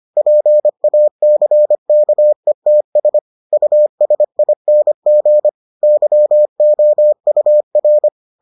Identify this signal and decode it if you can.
Morse code, heard as a keyed tone, about 25 wpm, 600 Hz. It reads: PACKETS USING YOUR